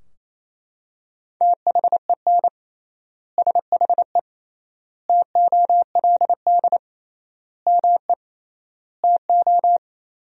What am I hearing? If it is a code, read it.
Morse code, heard as a keyed tone, about 28 wpm, 700 Hz. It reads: THEN SHE TOLD ME TO
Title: 3x Word-Spacing - Sentences from Top 400 Words - 28wpm